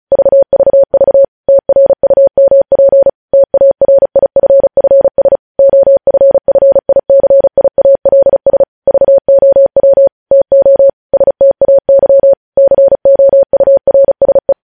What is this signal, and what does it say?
Morse code, heard as a keyed tone, about 36 wpm, 550 Hz. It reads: VVV TRUMP TARIFFS OFFICIALS VOW TO STAY COURSE
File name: podcast / News headlines-1x-35x35